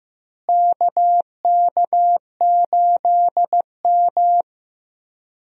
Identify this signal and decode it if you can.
Morse code, heard as a keyed tone, about 15 wpm, 700 Hz. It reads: KK8M
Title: USA Call Signs - 4 Characters - Repeated 3x - 15wpm